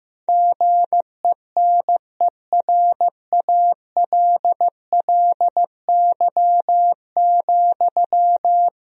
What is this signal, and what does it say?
Morse code, heard as a keyed tone, about 15 wpm, 700 Hz. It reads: GENERALLY,
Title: Sentences from Top 1000 Words - Review Part 4 - 15wpm